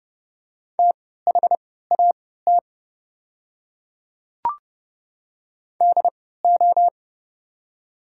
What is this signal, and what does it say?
Morse code, heard as a keyed tone, about 30 wpm, spaced as for 17 wpm, 700 Hz. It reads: THAT DO
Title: 3x Character Spacing - Top 100 Words - 30wpm